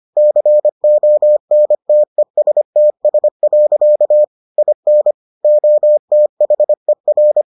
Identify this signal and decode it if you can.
Morse code, heard as a keyed tone, about 25 wpm, 600 Hz. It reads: CONTESTS. IN OTHER